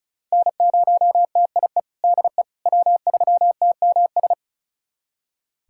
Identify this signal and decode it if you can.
Morse code, heard as a keyed tone, about 36 wpm, 700 Hz. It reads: N0TIE DE W3TMS
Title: USA Callsign DE Callsign - 5 Characters - 35wpm